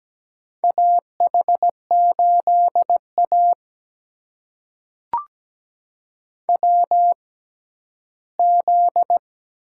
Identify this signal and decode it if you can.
Morse code, heard as a keyed tone, about 17 wpm, spaced as for 11 wpm, 700 Hz. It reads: AH8A WZ
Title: USA Call Signs - 4 Characters - 6x Character Spacing - 17wpm